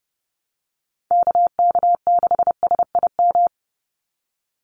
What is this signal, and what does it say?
Morse code, heard as a keyed tone, about 30 wpm, 700 Hz. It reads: KK6SIM
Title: USA Call Signs 30wpm